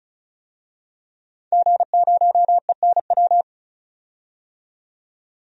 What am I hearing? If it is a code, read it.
Morse code, heard as a keyed tone, about 36 wpm, 700 Hz. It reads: G0ENW
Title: DXCC-100 Call Signs - 5 Characters - 35wpm